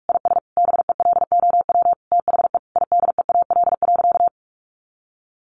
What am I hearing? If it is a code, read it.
Morse code, heard as a keyed tone, about 44 wpm, 700 Hz. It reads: IS BELOW THE IDEAL.
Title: Sentences from Top 1400 Words 45wpm